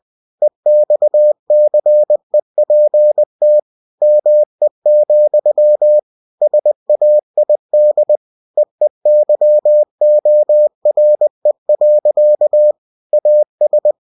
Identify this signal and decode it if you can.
Morse code, heard as a keyed tone, about 20 wpm, 600 Hz. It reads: EXCEPT ME, SAID EEYORE. AS
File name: cw-books / winnie-the-pooh / winnie-20wpm-page-0083